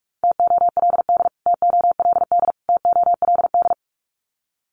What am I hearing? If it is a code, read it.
Morse code, heard as a keyed tone, about 44 wpm, 700 Hz. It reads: TOLD TOLD TOLD
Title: Top 500 Words Repeated 45wpm